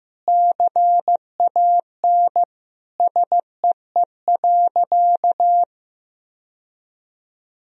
Morse code, heard as a keyed tone, about 15 wpm, 700 Hz. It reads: CAN SEE.
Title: Sentences from Top 1000 Words - Review Part 1 - 15wpm